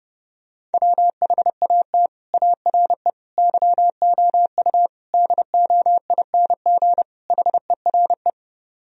Morse code, heard as a keyed tone, about 32 wpm, 700 Hz. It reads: WHAT ARE YOU DOING HERE